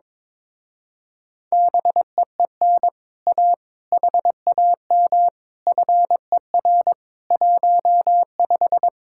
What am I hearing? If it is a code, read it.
Morse code, heard as a keyed tone, about 22 wpm, 700 Hz. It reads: BEEN A HAM FER 15
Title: Speed-Racing Common QSO Phrases - 22wpm